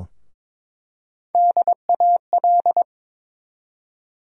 Morse code, heard as a keyed tone, about 22 wpm, 700 Hz. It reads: DAL
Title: USA Call Sign Postfixes - 3 Letters 22wpm